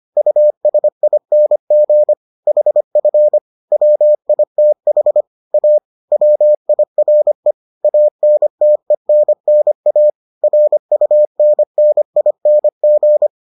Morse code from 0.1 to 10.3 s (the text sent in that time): USING HF WITH A WIRE ANTENNA